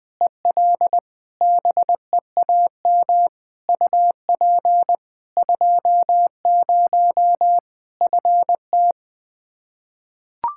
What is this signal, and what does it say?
Morse code, heard as a keyed tone, about 20 wpm, 700 Hz. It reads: EL BEAM UP 20 FT
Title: Practice QSO - Exchange 2 - Info - 20wpm